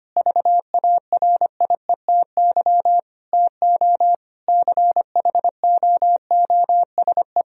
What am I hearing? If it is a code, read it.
Morse code, heard as a keyed tone, about 25 wpm, 700 Hz. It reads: VARIETY TO CHOOSE
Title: Sentences from Top 1000 Words - Review Part 4 - 25wpm